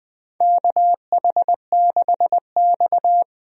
Morse code, heard as a keyed tone, about 20 wpm, 700 Hz. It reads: KH6X
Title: DXCC-100 Call Signs - 4 Characters - 20wpm